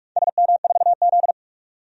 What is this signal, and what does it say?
Morse code, heard as a keyed tone, about 44 wpm, 700 Hz. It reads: SM4Z
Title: DXCC-100 Call Signs - 4 Characters - 45wpm